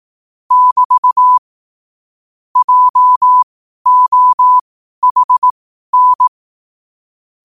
Morse code, heard as a keyed tone, about 18 wpm, spaced as for 12 wpm, 1000 Hz. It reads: <BT> JOHN
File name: qotdfiles / 12wpm